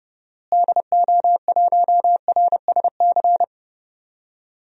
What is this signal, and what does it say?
Morse code, heard as a keyed tone, about 30 wpm, 700 Hz. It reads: DO1RSC
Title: DXCC-100 Call Signs - 6 Characters - Repeated 2x - 30wpm